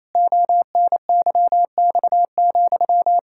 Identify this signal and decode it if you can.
Morse code, heard as a keyed tone, about 28 wpm, 700 Hz. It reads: ONYX,
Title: Pangram Challenge v2 - 28wpm